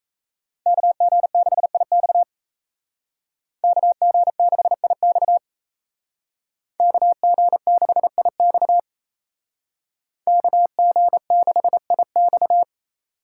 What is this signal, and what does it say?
Morse code, heard as a keyed tone, about 34 wpm, 700 Hz. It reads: KG6IX KG6IX KG6IX KG6IX
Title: Speed Racing - USA 2x2 Call Signs 28wpm